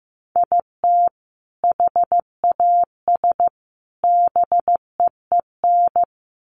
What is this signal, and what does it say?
Morse code, heard as a keyed tone, about 15 wpm, 700 Hz. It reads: IT HAS BEEN
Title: Sets of 3 Words 15wpm